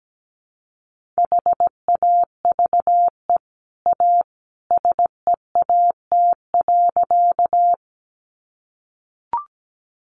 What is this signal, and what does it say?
Morse code, heard as a keyed tone, about 17 wpm, 700 Hz. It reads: HAVE A SEAT.
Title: Sentences from Top 1500 Words 17wpm